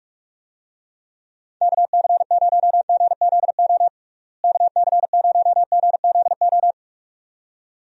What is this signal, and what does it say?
Morse code, heard as a keyed tone, about 43 wpm, 700 Hz. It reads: KC0GZO KC0GZO
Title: USA Call Signs - 6 Characters - Repeated 2x - 45wpm